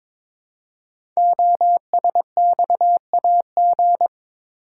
Morse code, heard as a keyed tone, about 22 wpm, 700 Hz. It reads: OSXAG
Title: Code Groups - 5 Letters - 22wpm